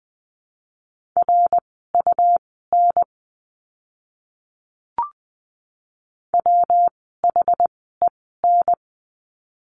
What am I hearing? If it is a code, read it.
Morse code, heard as a keyed tone, about 20 wpm, spaced as for 14 wpm, 700 Hz. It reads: RUN WHEN
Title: 115 Lesson 19f - Letter w - Words 20wpm